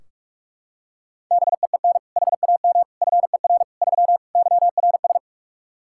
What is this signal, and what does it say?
Morse code, heard as a keyed tone, about 46 wpm, 700 Hz. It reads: BEEN HAM FER 3 YRS
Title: K1USN SST QSO Practice 45wpm